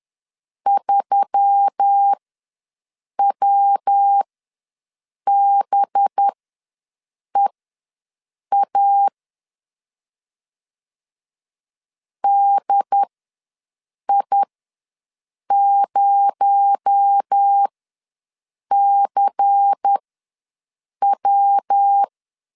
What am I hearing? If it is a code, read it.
Morse code, heard as a keyed tone, about 10 wpm, spaced as for 6 wpm, 800 Hz. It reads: 3WBEA DI0CW
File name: LEZIONI / 6 PKX0 / 6 PKX0 ws 10wpm 4220